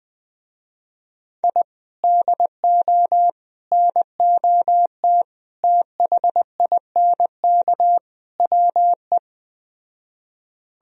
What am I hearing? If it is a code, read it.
Morse code, heard as a keyed tone, about 20 wpm, 700 Hz. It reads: I DO NOT THINK WE
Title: Speed-Racing - Sets of 5 Words 17wpm